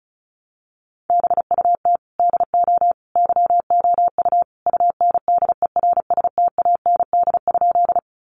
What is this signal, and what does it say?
Morse code, heard as a keyed tone, about 34 wpm, 700 Hz. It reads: BUT DO YOU UNDERSTAND?
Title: Sentences from Top 600 Words 35wpm